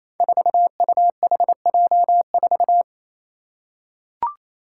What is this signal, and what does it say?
Morse code, heard as a keyed tone, about 28 wpm, 700 Hz. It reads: 4UHJ4
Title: Five Challenging Letter-Number Combinations 28wpm